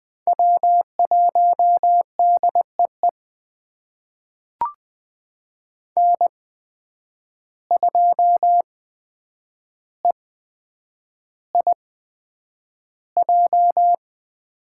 Morse code, heard as a keyed tone, about 20 wpm, spaced as for 9 wpm, 700 Hz. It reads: W1DEE N2EIJ
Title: USA Call Signs - 5 Characters - 8x Character Spacing - 20wpm